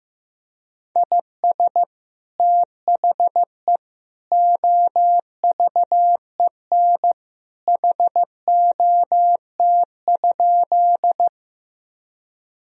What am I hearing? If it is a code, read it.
Morse code, heard as a keyed tone, about 15 wpm, 700 Hz. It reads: IS THE OVEN HOT?
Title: Sentences from Top 1000 Words - Review Part 4 - 15wpm